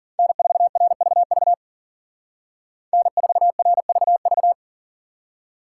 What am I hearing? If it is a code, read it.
Morse code, heard as a keyed tone, about 48 wpm, 700 Hz. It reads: N4RVV N4RVV
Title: Speed Racing - USA 1x3 Call Signs 40wpm